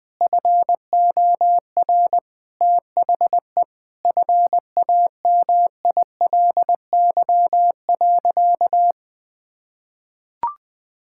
Morse code, heard as a keyed tone, about 21 wpm, 700 Hz. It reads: FOR THE FAMILY.
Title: Sentences from Top 1600 Words 20wpm